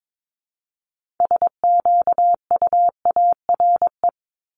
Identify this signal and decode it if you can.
Morse code, heard as a keyed tone, about 22 wpm, 700 Hz. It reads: SQUARE
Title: Multi-letter Phonemes 22wpm